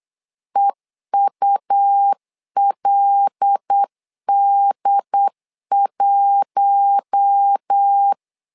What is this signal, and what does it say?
Morse code, heard as a keyed tone, about 8 wpm, 800 Hz. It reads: EULD1
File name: LEZIONI / 5 LBVC / 5 LB 8wpm 2816